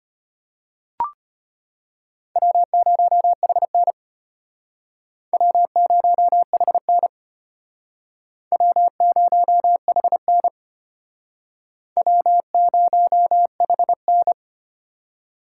Morse code, heard as a keyed tone, about 31 wpm, 700 Hz. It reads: W0HN W0HN W0HN W0HN
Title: Speed Racing - USA 1x2 Call Signs 25wpm